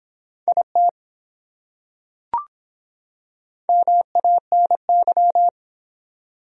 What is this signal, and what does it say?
Morse code, heard as a keyed tone, about 23 wpm, 700 Hz. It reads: IT MANY